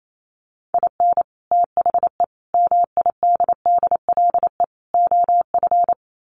Morse code, heard as a keyed tone, about 28 wpm, 700 Hz. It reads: IN THE MIDDLE OF
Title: Sets of 4 Words 28wpm